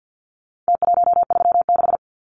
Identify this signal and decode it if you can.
Morse code, heard as a keyed tone, about 50 wpm, 700 Hz. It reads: T136
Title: Numbers - Contest Style - 4-Digit 50wpm